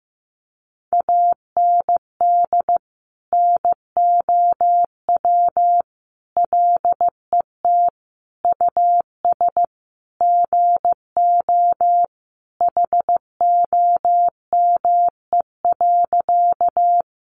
Morse code, heard as a keyed tone, about 15 wpm, 700 Hz. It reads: AND NOW LET US GO HOME.
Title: Sentences from Top 300 Words 15wpm